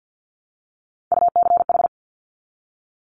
Morse code, heard as a keyed tone, about 50 wpm, 700 Hz. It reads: UCH
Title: Top Three Letter Combinations in Words 50wpm